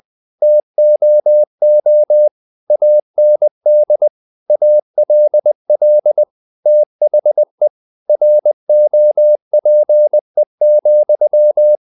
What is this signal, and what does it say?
Morse code, heard as a keyed tone, about 20 wpm, 600 Hz. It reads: TOO AND ALL THE ROPE,
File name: the old man and the sea part 2 20wpm